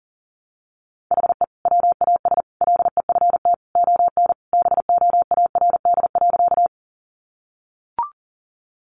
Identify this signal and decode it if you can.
Morse code, heard as a keyed tone, about 40 wpm, 700 Hz. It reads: HE WAS LEFT ON BOARD.
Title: Sentences from Top 700 Words 40wpm